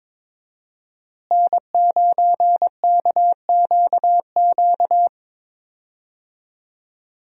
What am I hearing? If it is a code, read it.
Morse code, heard as a keyed tone, about 22 wpm, 700 Hz. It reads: N9KQQ
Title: USA Call Signs - 5 Characters - Repeated 2x - 22wpm